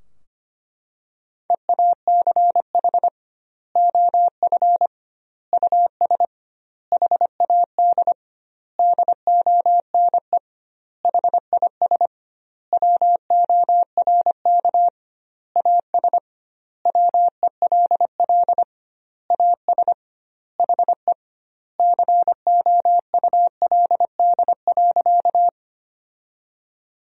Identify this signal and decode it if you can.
Morse code, heard as a keyed tone, about 25 wpm, 700 Hz. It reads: EACH OF US HAD DONE HIS WORK AS WELL AS HE COULD.
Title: 2x Word-Spacing - Sentences from Top 500 Words - 25wpm